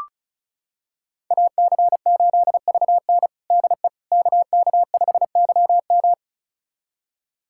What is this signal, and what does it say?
Morse code, heard as a keyed tone, about 36 wpm, 700 Hz. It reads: AC8VN DE KK5YM
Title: USA Callsign DE Callsign - 5 Characters - 35wpm